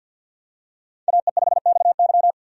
Morse code, heard as a keyed tone, about 48 wpm, 700 Hz. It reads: AE5XY
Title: USA Call Signs - 5 Characters - 50wpm